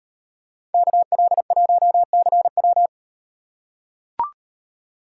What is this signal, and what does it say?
Morse code, heard as a keyed tone, about 38 wpm, 700 Hz. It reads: KL1CW